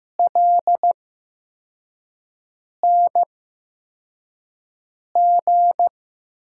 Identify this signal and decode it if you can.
Morse code, heard as a keyed tone, about 15 wpm, spaced as for 4 wpm, 700 Hz. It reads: LNG
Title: USA Call Signs - 6 Characters - 8x Character Spacing - 15wpm